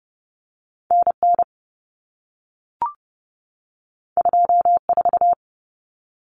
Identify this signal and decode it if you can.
Morse code, heard as a keyed tone, about 30 wpm, 700 Hz. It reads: NN 24
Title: Common Cut Numbers - 2-Digit 30wpm